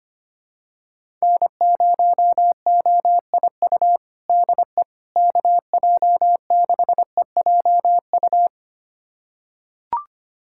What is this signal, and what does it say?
Morse code, heard as a keyed tone, about 25 wpm, 700 Hz. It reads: N0OIU DE KJ6EJU
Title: USA Callsign DE Callsign 25wpm